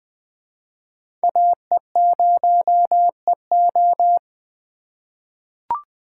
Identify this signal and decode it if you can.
Morse code, heard as a keyed tone, about 21 wpm, 700 Hz. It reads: AE0EO
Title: USA Call Signs - 5 Characters - 4x Character Spacing - 20wpm